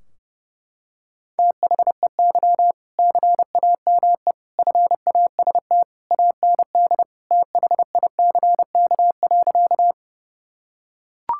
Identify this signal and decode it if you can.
Morse code, heard as a keyed tone, about 30 wpm, 700 Hz. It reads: THEY CAME FAST AND THICK.
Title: Sentences from Top 1600 Words 30wpm